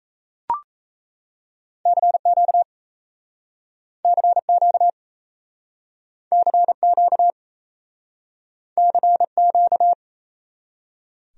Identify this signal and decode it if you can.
Morse code, heard as a keyed tone, about 38 wpm, 700 Hz. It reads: CQ CQ CQ CQ